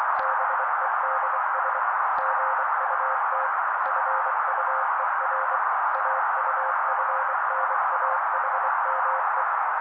Morse code, heard as a keyed tone, about 24 wpm, 530 Hz. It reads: BEDS GUT FUER AUFNAHME
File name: daily / 23wpm noise